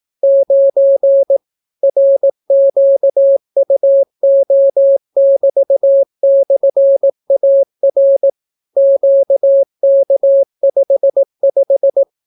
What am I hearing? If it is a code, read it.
Morse code, heard as a keyed tone, about 18 wpm, 550 Hz. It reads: 9 RQUO<BT>/AR QK55